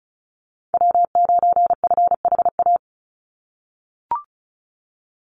Morse code, heard as a keyed tone, about 34 wpm, 700 Hz. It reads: W9FHA
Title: NA SOTA Chaser Callsigns 35wpm